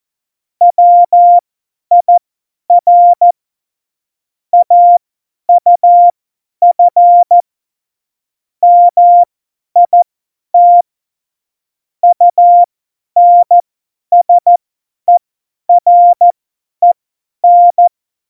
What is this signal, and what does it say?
Morse code, heard as a keyed tone, about 14 wpm, spaced as for 10 wpm, 700 Hz. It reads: WIR AUF MIT UNSEREN
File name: Morsekurs / Witze / Witz 07 April 2020, 50 F70